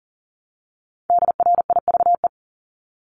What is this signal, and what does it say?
Morse code, heard as a keyed tone, about 40 wpm, 700 Hz. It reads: DRIVE
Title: Multi-letter Phonemes Repeated 3x 40wpm